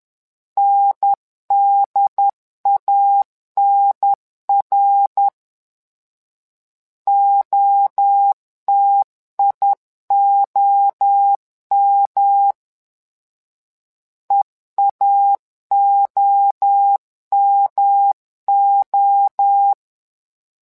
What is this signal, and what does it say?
Morse code, heard as a keyed tone, about 10 wpm, 800 Hz. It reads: NDANR OTIOM EAOMO
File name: LEZIONI / 3 MWRO / 3 MWRO 10wpm 2148